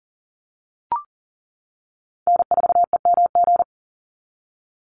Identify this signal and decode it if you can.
Morse code, heard as a keyed tone, about 40 wpm, 700 Hz. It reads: N4EMG
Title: NA SOTA Chaser Callsigns 40wpm